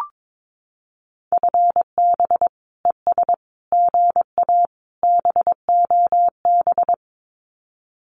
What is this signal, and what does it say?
Morse code, heard as a keyed tone, about 22 wpm, 700 Hz. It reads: FB ES GA BOB
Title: Common QSO Phrases Encore 22wpm